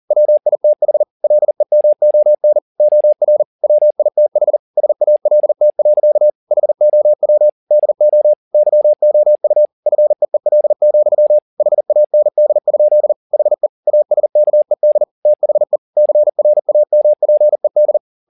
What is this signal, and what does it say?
Morse code, heard as a keyed tone, about 40 wpm, 600 Hz. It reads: WITH LEMON OR WITH SALT. HOW DO YOU FEEL, HAND? HE ASKED THE CRAMPED